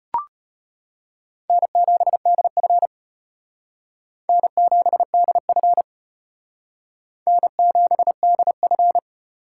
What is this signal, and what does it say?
Morse code, heard as a keyed tone, about 31 wpm, 700 Hz. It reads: N7DF N7DF N7DF
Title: Speed Racing - USA 1x2 Call Signs 25wpm